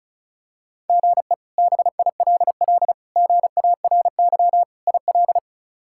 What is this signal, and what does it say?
Morse code, heard as a keyed tone, about 34 wpm, 700 Hz. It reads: GE BILL GARY IL
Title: K1USN SST QSO Practice - 35wpm